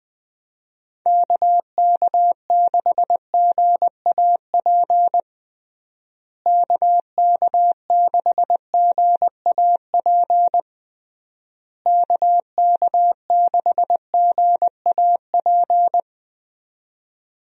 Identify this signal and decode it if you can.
Morse code, heard as a keyed tone, about 20 wpm, 700 Hz. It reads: KK6GAP KK6GAP KK6GAP
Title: USA Call Signs - 6 Characters - Repeated 3x - 20wpm